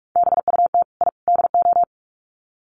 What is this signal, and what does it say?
Morse code, heard as a keyed tone, about 44 wpm, 700 Hz. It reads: BUT I DO
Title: Sets of 3 Words 45wpm